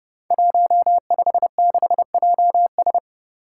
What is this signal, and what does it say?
Morse code, heard as a keyed tone, about 30 wpm, 700 Hz. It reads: 156JS
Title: Five Challenging Letter-Number Combinations 30wpm